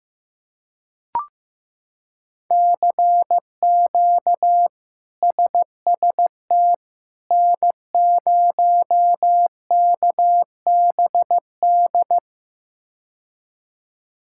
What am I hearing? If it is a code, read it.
Morse code, heard as a keyed tone, about 15 wpm, 700 Hz. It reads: CQ SST N0KBD
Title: K1USN SST QSO Practice - 15wpm